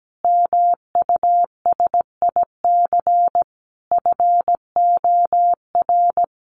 Morse code, heard as a keyed tone, about 17 wpm, 700 Hz. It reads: MUSIC FOR
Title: Sentences from Top 300 Words 17wpm